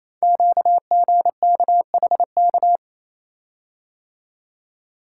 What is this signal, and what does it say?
Morse code, heard as a keyed tone, about 28 wpm, 700 Hz. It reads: QGKHK
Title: Code Groups - 5 Letters - 28wpm